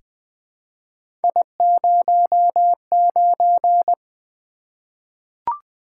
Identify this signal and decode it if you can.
Morse code, heard as a keyed tone, about 20 wpm, 700 Hz. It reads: I09
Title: Three Letter-Number Combinations - 20wpm